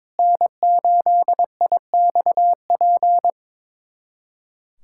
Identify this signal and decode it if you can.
Morse code, heard as a keyed tone, about 22 wpm, 700 Hz. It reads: N8IXP
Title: USA Callsign DE Callsign - 5 Characters - 22wpm